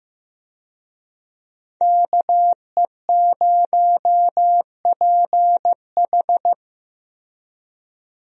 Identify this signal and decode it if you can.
Morse code, heard as a keyed tone, about 16 wpm, 700 Hz. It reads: KE0PH
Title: USA Call Signs - 5 Characters - Repeated 3x - 15wpm